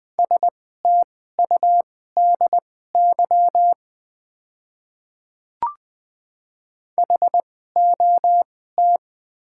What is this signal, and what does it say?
Morse code, heard as a keyed tone, about 20 wpm, spaced as for 15 wpm, 690 Hz. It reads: STUDY HOT
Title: 2x Character Spacing - Top 300 Words - 20wpm